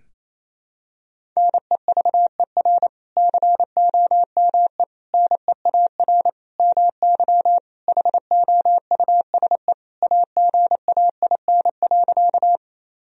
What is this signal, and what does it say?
Morse code, heard as a keyed tone, about 28 wpm, 700 Hz. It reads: NEVER COME NEAR MY HOUSE AGAIN.
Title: Sentences from Top 1000 Words - Review Part 1 - 28wpm